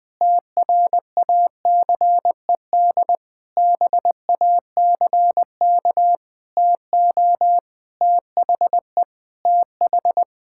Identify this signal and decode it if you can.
Morse code, heard as a keyed tone, about 20 wpm, 700 Hz. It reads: TRACED BACK TO THE TH
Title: Sentences with 6-Word Phrases - 20wpm